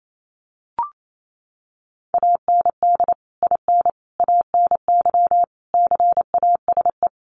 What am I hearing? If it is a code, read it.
Morse code, heard as a keyed tone, about 28 wpm, 700 Hz. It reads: AND IN ANY CASE
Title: Sentences from Top 1200 Words 28wpm